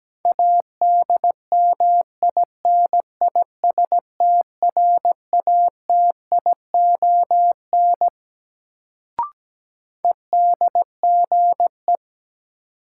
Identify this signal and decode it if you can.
Morse code, heard as a keyed tone, about 17 wpm, 700 Hz. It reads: ADMINISTRATION EDGE
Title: Sentences from Top 1300 Words - New Word Drill - 17wpm